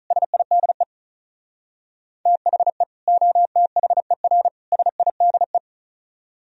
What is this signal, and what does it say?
Morse code, heard as a keyed tone, about 43 wpm, 700 Hz. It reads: SIDE THE OTHER SIDE
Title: Speed-Racing - Sets of 3 Words 35wpm